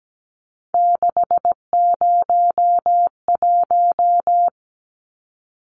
Morse code, heard as a keyed tone, about 17 wpm, 700 Hz. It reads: 601
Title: Numbers - 3-Digit 17wpm